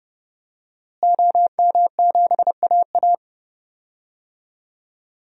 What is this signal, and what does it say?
Morse code, heard as a keyed tone, about 30 wpm, 700 Hz. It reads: OM7AA
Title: DXCC-100 Call Signs - 5 Characters - Repeated 3x - 30wpm